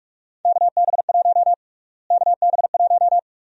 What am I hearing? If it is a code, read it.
Morse code, heard as a keyed tone, about 46 wpm, 700 Hz. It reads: KB1 KB1
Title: USA Call Sign Prefixes - Repeated 2x - 45wpm